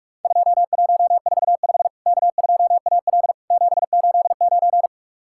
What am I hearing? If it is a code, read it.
Morse code, heard as a keyed tone, about 44 wpm, 700 Hz. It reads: 2145 K2AL 789
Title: Worked All Europe DX Contest - QTC Exchange - 45wpm